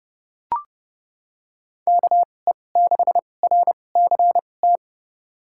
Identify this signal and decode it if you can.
Morse code, heard as a keyed tone, about 30 wpm, spaced as for 22 wpm, 700 Hz. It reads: KE6RCT
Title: USA Call Signs - 2x Character Spacing - 30wpm